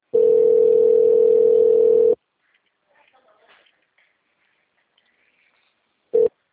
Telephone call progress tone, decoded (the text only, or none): ringback tone